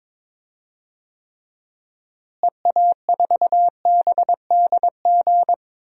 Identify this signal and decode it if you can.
Morse code, heard as a keyed tone, about 21 wpm, 700 Hz. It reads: EA4BDG
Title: DXCC-100 Call Signs - 6 Characters - Repeated 3x - 22wpm